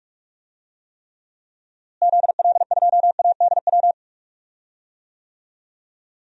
Morse code, heard as a keyed tone, about 45 wpm, 700 Hz. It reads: ZL2ADW
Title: DXCC-100 Call Signs - 6 Characters - Repeated 3x - 45wpm